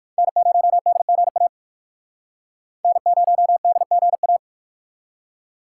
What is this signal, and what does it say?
Morse code, heard as a keyed tone, about 62 wpm, 700 Hz. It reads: N0DGA N0DGA